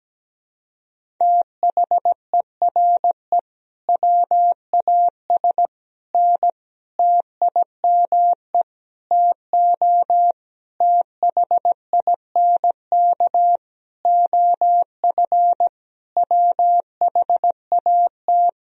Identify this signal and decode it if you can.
Morse code, heard as a keyed tone, about 17 wpm, 700 Hz. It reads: THERE WAS N TIME TO THINK OF WHAT
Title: Sentences from Top 400 Words and Abbreviations 17wpm